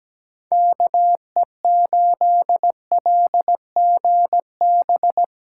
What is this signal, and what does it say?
Morse code, heard as a keyed tone, about 17 wpm, 700 Hz. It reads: KE8LGB
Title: USA Callsign DE Callsign - 6 Characters - 17wpm